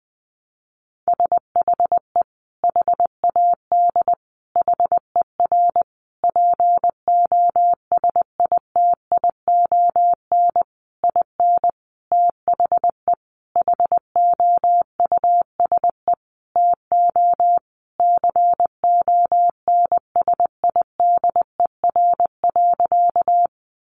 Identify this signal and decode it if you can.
Morse code, heard as a keyed tone, about 20 wpm, 700 Hz. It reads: SHE HAD HER POSITION IN THE HOUSE TO CONSIDER.
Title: Sentences from Top 900 Words 20wpm